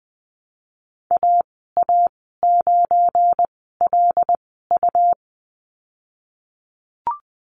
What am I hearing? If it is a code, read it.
Morse code, heard as a keyed tone, about 20 wpm, spaced as for 15 wpm, 700 Hz. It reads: AA9LU
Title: 200 Lesson 32g - Letter k - Call Signs 20wpm